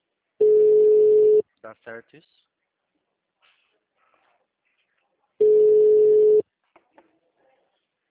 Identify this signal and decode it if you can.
Ringback tone (Europe, 425 Hz), with speech.